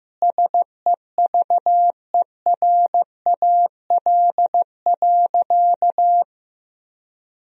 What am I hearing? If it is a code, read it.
Morse code, heard as a keyed tone, about 15 wpm, 700 Hz. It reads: SEVERAL.
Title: Sentences from Top 1000 Words - Review Part 2 - 15wpm